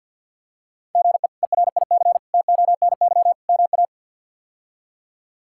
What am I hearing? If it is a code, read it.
Morse code, heard as a keyed tone, about 53 wpm, 700 Hz. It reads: GE ERIC TONY MA
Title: K1USN SST QSO Practice - 50wpm